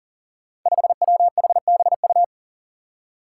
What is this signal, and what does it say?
Morse code, heard as a keyed tone, about 40 wpm, 700 Hz. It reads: 5WHBU
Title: Five Challenging Letter-Number Combinations 40wpm